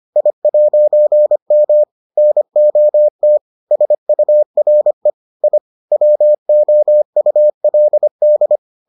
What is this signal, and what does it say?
Morse code, heard as a keyed tone, about 25 wpm, 600 Hz. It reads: I'M NOT SURE I WOULD